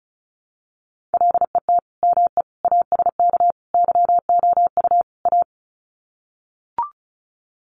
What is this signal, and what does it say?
Morse code, heard as a keyed tone, about 34 wpm, 700 Hz. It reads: LET ME ASK YOU A
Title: Sets of 5 Words Encore 35wpm